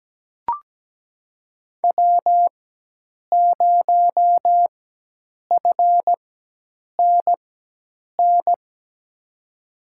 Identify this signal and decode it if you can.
Morse code, heard as a keyed tone, about 17 wpm, spaced as for 8 wpm, 700 Hz. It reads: W0FNN
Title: USA Call Signs - 5 Characters - 4x Character Spacing - 17wpm